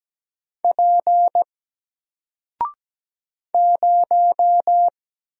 Morse code, heard as a keyed tone, about 17 wpm, 700 Hz. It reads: P 0
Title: Single Letter-Number 17wpm